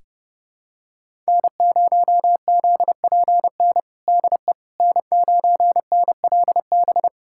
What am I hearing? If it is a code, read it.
Morse code, heard as a keyed tone, about 30 wpm, 700 Hz. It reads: N0ZPN DE N9NLB
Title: USA Callsign DE Callsign 30wpm